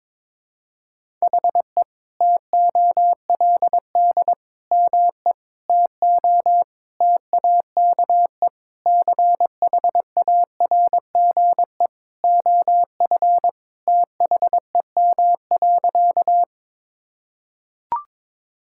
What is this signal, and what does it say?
Morse code, heard as a keyed tone, about 21 wpm, 700 Hz. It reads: HE TOLD ME TO TAKE CHARGE OF THEM.
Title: Sentences from Top 1000 Words - Review Part 6 - 22wpm